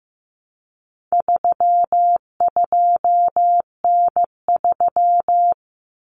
Morse code, heard as a keyed tone, about 15 wpm, 700 Hz. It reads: 32N3
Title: Common Cut Numbers - 4-Digit 15wpm